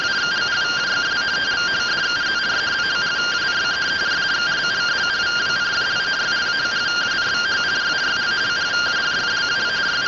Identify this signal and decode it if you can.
Two-tone teleprinter signal, about 170 Hz shift.
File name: findings / CT 1 Scout 0002.gen-17 rtty rx